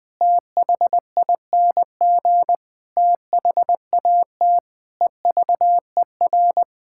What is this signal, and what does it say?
Morse code, heard as a keyed tone, about 20 wpm, 700 Hz. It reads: THING THAT EVER
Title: Sets of 8 Words - 20wpm